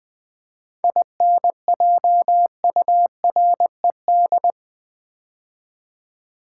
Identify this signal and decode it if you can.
Morse code, heard as a keyed tone, about 20 wpm, 700 Hz. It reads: INJURED
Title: 7 Letter Words - First 3 Letters Given - 20wpm